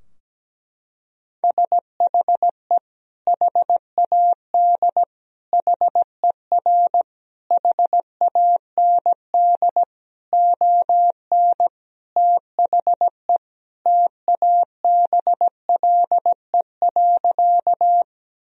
Morse code, heard as a keyed tone, about 17 wpm, 700 Hz. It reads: SHE HAD HER HAND ON THE TABLE.
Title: Sentences from Top 1000 Words - Review Part 2 - 17wpm